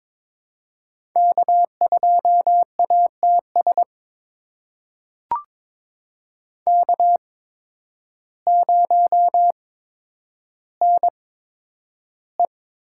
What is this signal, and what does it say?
Morse code, heard as a keyed tone, about 22 wpm, spaced as for 10 wpm, 700 Hz. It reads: K2ATS K0NE
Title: USA Call Signs - 8x Character Spacing - 22wpm